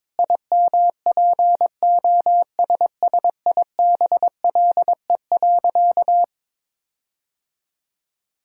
Morse code, heard as a keyed tone, about 22 wpm, 700 Hz. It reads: IMPOSSIBLE.
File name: practice / 22wpm / Sentences with 4-Word Phrases - 22wpm